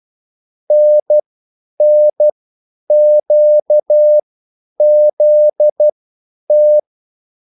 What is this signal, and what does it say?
Morse code, heard as a keyed tone, about 12 wpm, spaced as for 9 wpm, 600 Hz. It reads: NNQZT